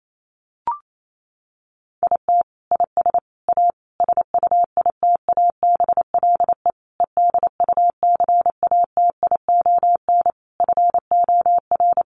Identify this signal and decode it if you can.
Morse code, heard as a keyed tone, about 28 wpm, 700 Hz. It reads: IT IS A SUITABLE EDUCATION FOR
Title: Sentences from Top 1400 Words 28wpm